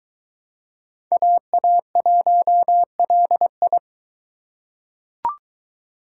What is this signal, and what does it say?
Morse code, heard as a keyed tone, about 21 wpm, 700 Hz. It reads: AA1LI